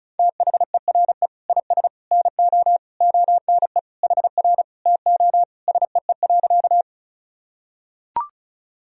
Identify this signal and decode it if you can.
Morse code, heard as a keyed tone, about 36 wpm, 700 Hz. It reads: THERE IS NO ONE HR TO SEE.
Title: Sentences from Top 200 Words and Abbreviations 35wpm